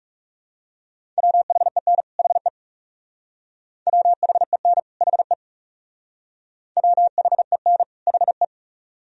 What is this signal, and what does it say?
Morse code, heard as a keyed tone, about 38 wpm, 700 Hz. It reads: WHEN HE WHEN HE WHEN HE